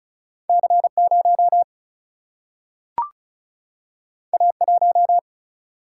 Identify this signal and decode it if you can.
Morse code, heard as a keyed tone, about 36 wpm, 700 Hz. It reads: C 0 A 1
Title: Single-Letter Single-Number 35wpm